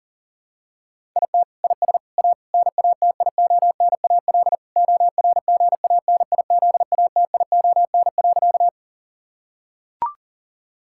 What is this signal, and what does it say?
Morse code, heard as a keyed tone, about 40 wpm, 700 Hz. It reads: IT IS A NATIONAL ORGANIZATION.
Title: Sentences from Top 1000 Words - Review Part 5 - 40wpm